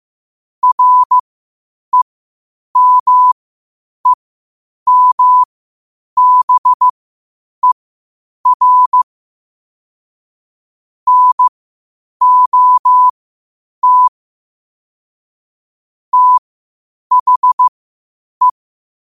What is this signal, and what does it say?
Morse code, heard as a keyed tone, about 15 wpm, spaced as for 8 wpm, 1000 Hz. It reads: REMEMBER NOT THE
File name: qotdfiles / 8wpm